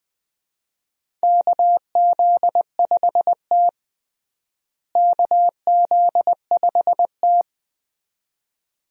Morse code, heard as a keyed tone, about 20 wpm, 700 Hz. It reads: KZ5T KZ5T
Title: USA Call Signs - 4 Characters - Repeated 2x - 20wpm